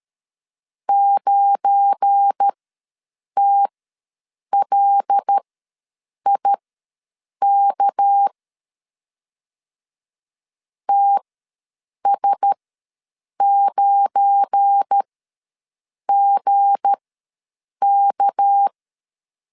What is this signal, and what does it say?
Morse code, heard as a keyed tone, about 13 wpm, spaced as for 7 wpm, 800 Hz. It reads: 9TLIK TS9GK
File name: LEZIONI / 7 GF9Z / 7 GF9 ws 12wpm 3344